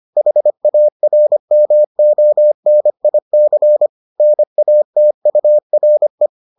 Morse code, heard as a keyed tone, about 25 wpm, 600 Hz. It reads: HARMONIC NATURE